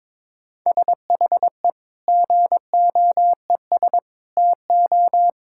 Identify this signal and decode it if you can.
Morse code, heard as a keyed tone, about 22 wpm, 700 Hz. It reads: SHE GOES TO
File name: Difficult Sentences with Hints - 22wpm